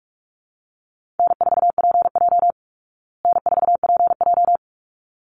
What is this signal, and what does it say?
Morse code, heard as a keyed tone, about 44 wpm, 700 Hz. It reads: N4PJ N4PJ
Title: Call Signs Repeated 3x - 45wpm